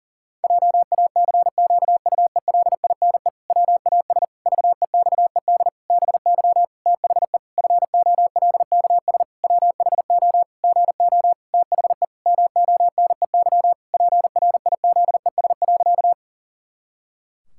Morse code, heard as a keyed tone, about 40 wpm, 700 Hz. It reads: JACQUELINE WAS VEXED BY THE FOLKS WHO GO THE MONEY PRIZES.
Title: Pangram Challenge v2 - 40wpm